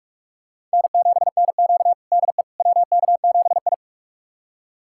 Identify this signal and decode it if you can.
Morse code, heard as a keyed tone, about 45 wpm, 700 Hz. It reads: N7NQ DE WK7I
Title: USA Callsign DE Callsign - 4 Characters - 45wpm